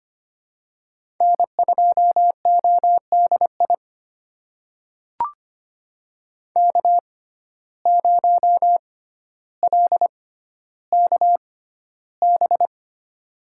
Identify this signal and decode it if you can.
Morse code, heard as a keyed tone, about 25 wpm, spaced as for 13 wpm, 700 Hz. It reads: N2ODI K0LKB
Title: USA Call Signs - 5 Characters - 6x Character Spacing - 25wpm